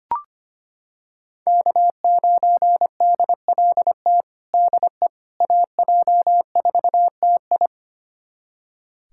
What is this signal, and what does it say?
Morse code, heard as a keyed tone, about 25 wpm, 700 Hz. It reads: K9DLT DE AJ4TI
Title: USA Callsign DE Callsign - 5 Characters - 25wpm